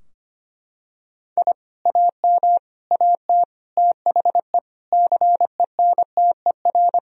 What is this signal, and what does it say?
Morse code, heard as a keyed tone, about 25 wpm, 700 Hz. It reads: I AM AT THE CENTER
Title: Sentences from Top 1000 Words - Review Part 2 - 25wpm